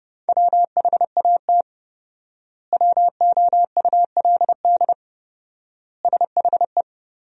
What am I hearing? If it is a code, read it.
Morse code, heard as a keyed tone, about 32 wpm, 700 Hz. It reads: WHAT WOULD SHE
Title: 4x Word-Spacing - Sentences from Top 100 Words - 30wpm